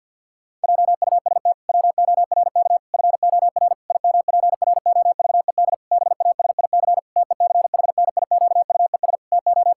Morse code, heard as a keyed tone, about 50 wpm, 700 Hz. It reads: JUST WORK FOR IMPROVED BASIC TECHNIQUES TO
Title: Pangram Challenge v2 - 50wpm